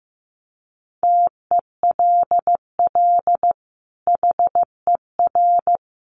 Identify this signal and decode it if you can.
Morse code, heard as a keyed tone, about 15 wpm, 700 Hz. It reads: TELL HER
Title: Sentences from Top 1200 Words 15wpm